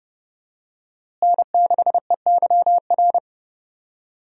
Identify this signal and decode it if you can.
Morse code, heard as a keyed tone, about 30 wpm, 700 Hz. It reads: N6EYR
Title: USA Call Signs - 5 Characters - Repeated 3x - 30wpm